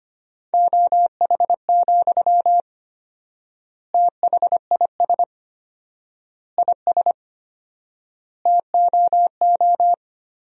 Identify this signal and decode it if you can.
Morse code, heard as a keyed tone, about 25 wpm, 700 Hz. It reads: OH, THIS IS TOO
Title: 4x Word-Spacing - Sentences from Top 500 Words - 25wpm